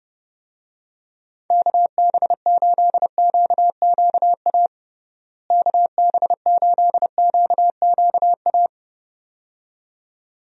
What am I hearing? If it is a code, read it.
Morse code, heard as a keyed tone, about 32 wpm, 700 Hz. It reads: KB8QQA KB8QQA
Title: USA Call Signs - 6 Characters - Repeated 2x - 30wpm